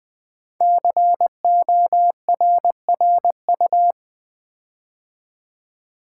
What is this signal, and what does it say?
Morse code, heard as a keyed tone, about 20 wpm, 700 Hz. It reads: CORRU
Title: Build A Word - 5 Letters At A Time - 10-Letter Words - 20wpm